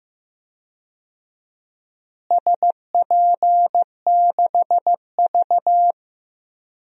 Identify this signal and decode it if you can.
Morse code, heard as a keyed tone, about 15 wpm, 700 Hz. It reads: SP6V
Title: DXCC-100 Call Signs - 4 Characters - Repeated 3x - 15wpm